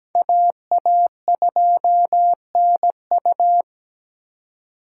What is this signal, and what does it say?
Morse code, heard as a keyed tone, about 20 wpm, 700 Hz. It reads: AA2NU
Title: Speed Racing - USA 2x2 Call Signs 17wpm